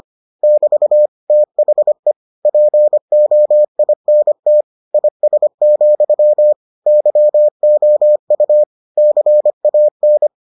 Morse code, heard as a keyed tone, about 25 wpm, 600 Hz. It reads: <BT> THE POINT IS, YOU CAN